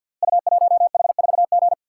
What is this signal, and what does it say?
Morse code, heard as a keyed tone, about 52 wpm, 690 Hz. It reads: U1H4G
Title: Code Groups - 5 Letter:Numbers - 50wpm